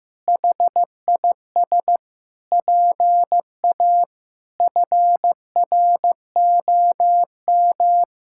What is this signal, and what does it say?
Morse code, heard as a keyed tone, about 15 wpm, 700 Hz. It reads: HIS PA FROM
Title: Pangram Challenge v2 - 15wpm